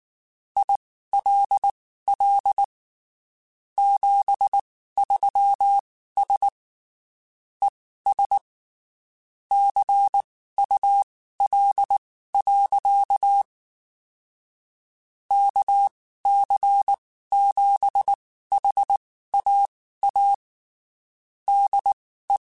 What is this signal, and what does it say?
Morse code, heard as a keyed tone, about 19 wpm, spaced as for 14 wpm, 800 Hz. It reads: ILL 73S ES CUL. KC7HAA DE
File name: (spaced) / QSOs Spaced-mp3s / kc7haa @18 sp